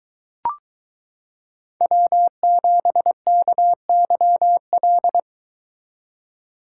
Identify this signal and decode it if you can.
Morse code, heard as a keyed tone, about 23 wpm, 700 Hz. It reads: W7KYL